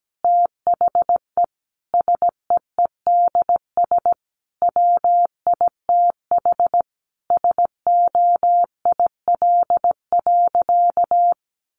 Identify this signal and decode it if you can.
Morse code, heard as a keyed tone, about 17 wpm, 700 Hz. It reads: THE SEEDS WITH SOIL.
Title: Easy Phrases 17wpm